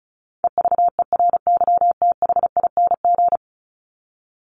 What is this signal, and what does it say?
Morse code, heard as a keyed tone, about 34 wpm, 700 Hz. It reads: EVERYTHING
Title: Top 501-750 Words 35wpm